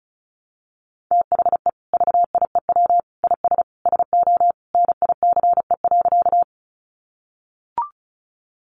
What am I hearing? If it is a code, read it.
Morse code, heard as a keyed tone, about 34 wpm, 700 Hz. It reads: THE VIEW IS SO NICE.
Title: Additional Easy Phrases 35wpm